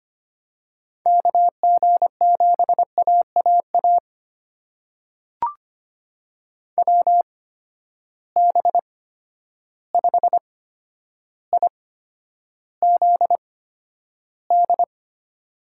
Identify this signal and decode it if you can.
Morse code, heard as a keyed tone, about 25 wpm, spaced as for 11 wpm, 700 Hz. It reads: KG7AAA WB5IZD
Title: USA Call Signs - 6 Characters - 8x Character Spacing - 25wpm